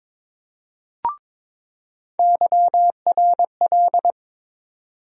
Morse code, heard as a keyed tone, about 22 wpm, 700 Hz. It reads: YRL
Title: USA Call Sign Postfixes - 3 Letters 22wpm